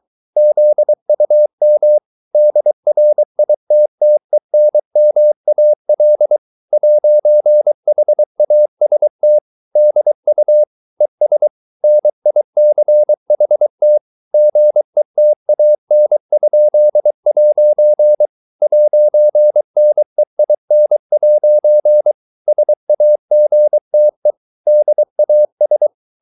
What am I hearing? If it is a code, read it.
Morse code, heard as a keyed tone, about 23 wpm, 600 Hz. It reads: ZUM DRITTENMAL 'HAST DU ES NICHT GETAN?' 'NEIN' SAGTE DAS